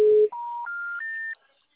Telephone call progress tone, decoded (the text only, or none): special information tone